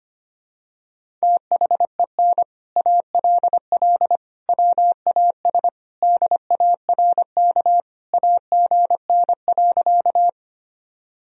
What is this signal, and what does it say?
Morse code, heard as a keyed tone, about 25 wpm, 700 Hz. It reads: THEN ALL WAS DARK AGN.
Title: Sentences from Top 500 Words and Abbreviations 25wpm